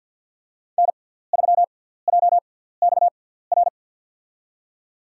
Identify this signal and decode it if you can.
Morse code, heard as a keyed tone, about 52 wpm, spaced as for 17 wpm, 700 Hz. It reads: N3JXR
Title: USA Call Signs - 5 Characters - 6x Character Spacing - 50wpm